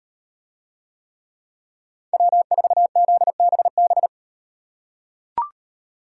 Morse code, heard as a keyed tone, about 38 wpm, 700 Hz. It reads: W4ZBB